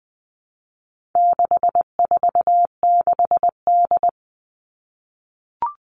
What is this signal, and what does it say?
Morse code, heard as a keyed tone, about 20 wpm, 700 Hz. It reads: 646D
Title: All Cut Numbers - 4-Digit 20wpm